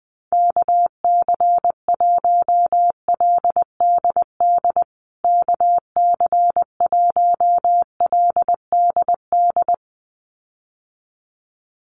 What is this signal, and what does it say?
Morse code, heard as a keyed tone, about 19 wpm, 700 Hz. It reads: KC1LDD KC1LDD
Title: USA Call Signs Repeated 2x 20wpm